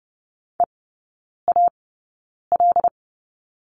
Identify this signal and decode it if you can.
Morse code, heard as a keyed tone, about 30 wpm, spaced as for 9 wpm, 700 Hz. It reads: EAL
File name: 081 Lesson 14e - Number 2 - Three Characters 30wpm